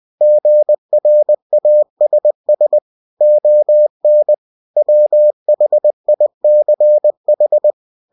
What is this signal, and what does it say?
Morse code, heard as a keyed tone, about 20 wpm, 600 Hz. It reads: GRASS ON WHICH
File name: cw-books / winnie-the-pooh / winnie-20wpm-page-0083